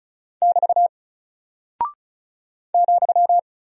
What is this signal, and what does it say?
Morse code, heard as a keyed tone, about 36 wpm, 700 Hz. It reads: <BT> ,
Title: Single Character (Letter:Number:Punctuation:Prosign) Alt 35wpm